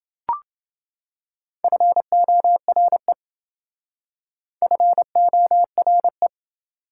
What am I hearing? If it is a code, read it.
Morse code, heard as a keyed tone, about 29 wpm, 700 Hz. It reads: FORE FORE